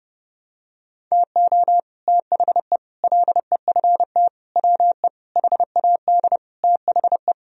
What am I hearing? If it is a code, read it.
Morse code, heard as a keyed tone, about 30 wpm, 700 Hz. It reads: TO THE LEFT WE HAD THE
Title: Sentences from Top 1000 Words - Review Part 1 - 30wpm